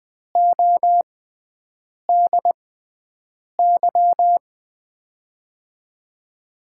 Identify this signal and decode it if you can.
Morse code, heard as a keyed tone, about 20 wpm, spaced as for 7 wpm, 700 Hz. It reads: ODY
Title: USA Call Signs - 5 Characters - 6x Character Spacing - 20wpm